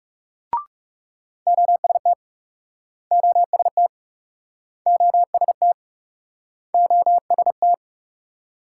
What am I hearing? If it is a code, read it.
Morse code, heard as a keyed tone, about 43 wpm, 700 Hz. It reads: OST OST OST OST